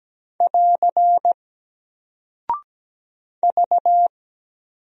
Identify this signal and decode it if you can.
Morse code, heard as a keyed tone, about 17 wpm, 700 Hz. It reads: <AR> V
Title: Single Character (Letter:Number:Punctuation:Prosign) Alt 17wpm